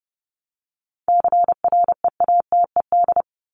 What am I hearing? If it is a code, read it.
Morse code, heard as a keyed tone, about 30 wpm, 700 Hz. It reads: CREATED
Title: ED Suffix - Top 100 Words with Base Words 30wpm